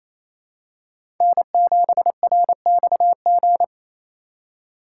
Morse code, heard as a keyed tone, about 28 wpm, 700 Hz. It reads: N7RXG
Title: USA Call Signs - 5 Characters - 4x Character Spacing - 28wpm